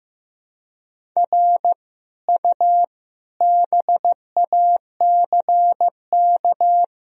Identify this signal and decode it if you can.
Morse code, heard as a keyed tone, about 15 wpm, 700 Hz. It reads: R U BACK
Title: Sentences from Top 200 Words and Abbreviations 15wpm